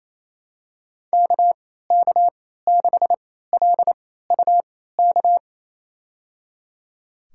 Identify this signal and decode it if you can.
Morse code, heard as a keyed tone, about 28 wpm, spaced as for 16 wpm, 700 Hz. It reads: KK6LUK
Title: USA Call Signs - 6 Characters - 3x Character Spacing - 28wpm